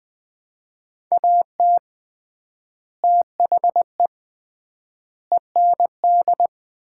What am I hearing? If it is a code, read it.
Morse code, heard as a keyed tone, about 21 wpm, 700 Hz. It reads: AT THE END
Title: 3x Word-Spacing - Sets of 5 Words - 20wpm